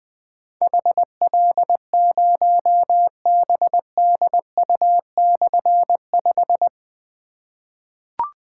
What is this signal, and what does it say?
Morse code, heard as a keyed tone, about 20 wpm, 700 Hz. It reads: HL0BDU/5
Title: Difficult Call Signs 20wpm